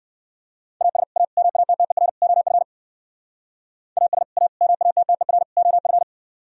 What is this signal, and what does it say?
Morse code, heard as a keyed tone, about 56 wpm, 700 Hz. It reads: AS A MATTER OF AS A MATTER OF
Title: Speed-Racing - Sets of 4 Words 45wpm